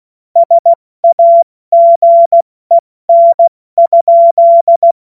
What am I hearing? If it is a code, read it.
Morse code, heard as a keyed tone, about 16 wpm, 680 Hz. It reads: SAGEN?
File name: Morsekurs / Witze / Witz 10. August 2021, 70 F80